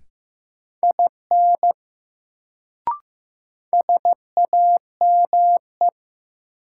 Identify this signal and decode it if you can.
Morse code, heard as a keyed tone, about 15 wpm, 700 Hz. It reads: IN SAME
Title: Sets of 2 Words - New Word Drill - 15wpm